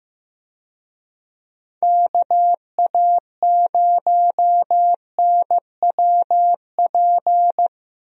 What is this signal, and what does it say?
Morse code, heard as a keyed tone, about 15 wpm, 700 Hz. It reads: KA0NWP
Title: USA Call Signs - 6 Characters - Repeated 2x - 15wpm